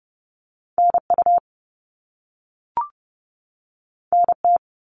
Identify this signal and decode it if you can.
Morse code, heard as a keyed tone, about 30 wpm, 700 Hz. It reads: NU NT
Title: Canadian Provinces and Territories 30wpm